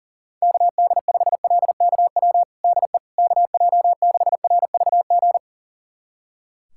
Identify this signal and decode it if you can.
Morse code, heard as a keyed tone, about 40 wpm, 700 Hz. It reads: KD5LKW DE KJ6RVG
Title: USA Callsign DE Callsign - 6 Characters - 40wpm